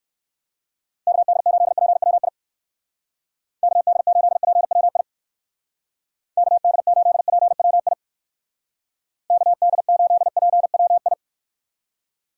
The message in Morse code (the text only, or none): KD8PWI KD8PWI KD8PWI KD8PWI